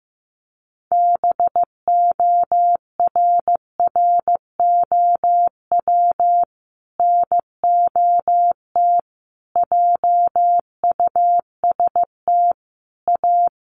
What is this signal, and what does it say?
Morse code, heard as a keyed tone, about 15 wpm, 700 Hz. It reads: BORROW NOT JUST A
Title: Sentences from Top 1500 Words 15wpm